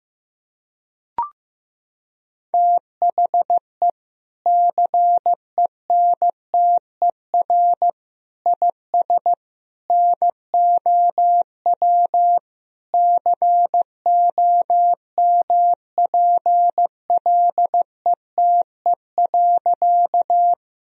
Morse code, heard as a keyed tone, about 15 wpm, 700 Hz. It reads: THE CENTER IS NOW COMPLETE.
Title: Sentences from Top 1000 Words - Review Part 2 - 15wpm